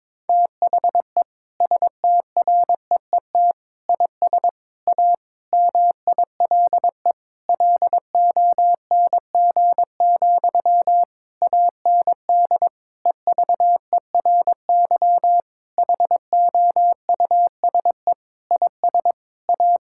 Morse code, heard as a keyed tone, about 22 wpm, 700 Hz. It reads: THE STREET IS A MILE LONG, AND EVERY HOUSE IS A
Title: Sentences from Top 1600 Words 22wpm